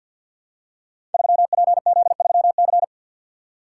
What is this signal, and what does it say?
Morse code, heard as a keyed tone, about 47 wpm, 700 Hz. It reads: 3PZ3C
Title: Code Groups - 5 Letter:Numbers - 50wpm